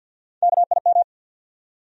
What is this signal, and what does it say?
Morse code, heard as a keyed tone, about 51 wpm, 700 Hz. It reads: KIM
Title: USA Common Names - Max 3 Letters - 50wpm